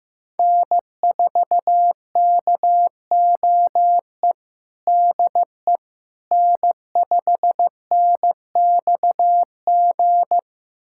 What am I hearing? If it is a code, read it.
Morse code, heard as a keyed tone, about 16 wpm, 700 Hz. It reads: N4KOE DE N5NXG
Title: USA Callsign DE Callsign - 5 Characters - 15wpm